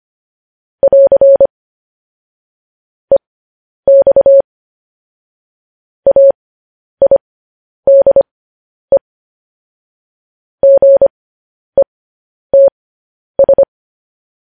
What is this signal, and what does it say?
Morse code, heard as a keyed tone, about 25 wpm, spaced as for 10 wpm, 550 Hz. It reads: <AR> EX AIDE GETS